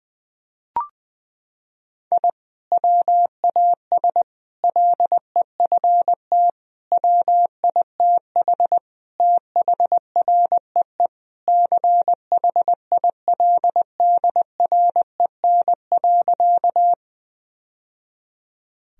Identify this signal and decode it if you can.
Morse code, heard as a keyed tone, about 21 wpm, 700 Hz. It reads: I WAS LEFT WITH THREE CHILDREN.
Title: Sentences from Top 1000 Words - Review Part 1 - 20wpm